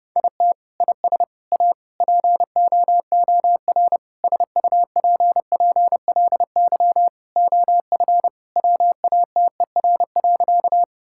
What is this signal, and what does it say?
Morse code, heard as a keyed tone, about 30 wpm, 700 Hz. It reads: IT IS A POOR SUPPLY OF WATER.
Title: Sentences from Top 1000 Words - Review Part 5 - 30wpm